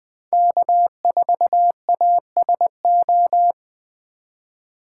Morse code, heard as a keyed tone, about 24 wpm, 700 Hz. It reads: K4ASO
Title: Speed Racing - USA 1x3 Call Signs 20wpm